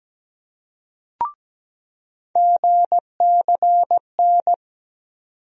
Morse code, heard as a keyed tone, about 17 wpm, 700 Hz. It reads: GCN
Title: USA Call Sign Postfixes - 3 Letters 17wpm